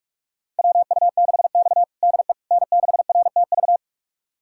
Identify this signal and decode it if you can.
Morse code, heard as a keyed tone, about 42 wpm, 700 Hz. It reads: WU6X DE N6RTV
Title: USA Callsign DE Callsign 45wpm